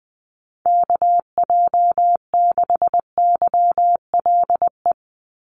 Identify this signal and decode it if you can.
Morse code, heard as a keyed tone, about 20 wpm, 700 Hz. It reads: KJ6YLE
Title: USA Call Signs 20wpm